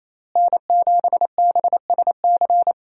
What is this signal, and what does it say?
Morse code, heard as a keyed tone, about 28 wpm, 700 Hz. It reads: N7BSC
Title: USA Call Signs - 5 Characters - Repeated 2x - 28wpm